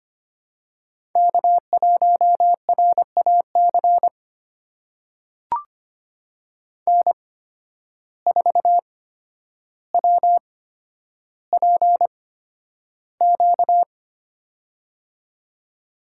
Morse code, heard as a keyed tone, about 25 wpm, spaced as for 11 wpm, 700 Hz. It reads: K1RAC N4WPQ
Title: USA Call Signs - 5 Characters - 8x Character Spacing - 25wpm